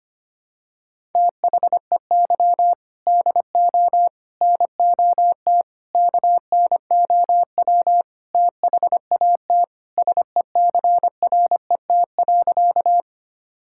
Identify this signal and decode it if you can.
Morse code, heard as a keyed tone, about 25 wpm, 700 Hz. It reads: THEY DO NOT KNOW THAT SECRET.
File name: practice / 25wpm / Sentences with 4-Word Phrases - 25wpm